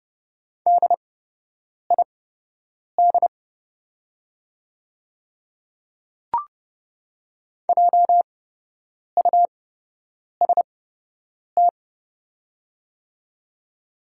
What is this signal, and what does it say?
Morse code, heard as a keyed tone, about 30 wpm, spaced as for 8 wpm, 700 Hz. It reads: DID JUST
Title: 8x Character Spacing - Top 200 Words - 30wpm